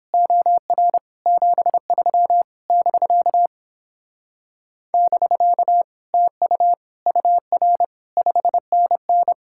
Morse code, heard as a keyed tone, about 29 wpm, 700 Hz. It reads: OR 73 <BK> <BK> TU UR 5NN